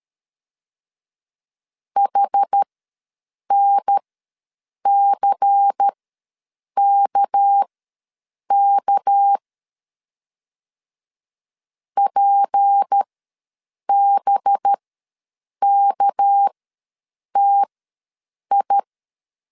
Morse code, heard as a keyed tone, about 13 wpm, spaced as for 7 wpm, 800 Hz. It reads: HNCKK PBKTI
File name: LEZIONI / 6 PKX0 / 6 PK ws 12wpm 3116